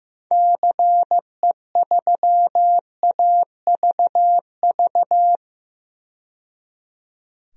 Morse code, heard as a keyed tone, about 16 wpm, 700 Hz. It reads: CE3AVV
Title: DXCC-100 Call Signs - 6 Characters - 15wpm